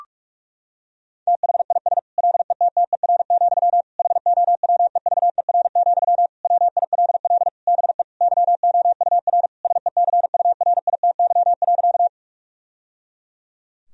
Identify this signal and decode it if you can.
Morse code, heard as a keyed tone, about 41 wpm, 700 Hz. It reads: THIS LETTER, HOWEVER, WILL BE YOUR SECURITY.
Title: Sentences from Top 1000 Words - Review Part 4 - 45wpm